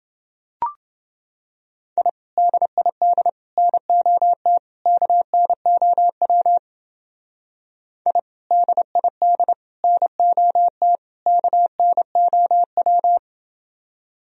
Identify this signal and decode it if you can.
Morse code, heard as a keyed tone, about 31 wpm, 700 Hz. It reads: I DID NOT KNOW I DID NOT KNOW